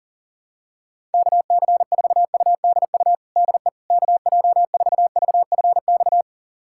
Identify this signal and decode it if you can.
Morse code, heard as a keyed tone, about 40 wpm, 700 Hz. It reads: KC4UDU DE KJ4VFX
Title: USA Callsign DE Callsign - 6 Characters - 40wpm